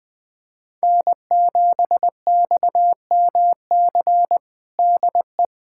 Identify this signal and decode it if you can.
Morse code, heard as a keyed tone, about 20 wpm, 700 Hz. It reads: N7XMC DE
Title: USA Callsign DE Callsign 20wpm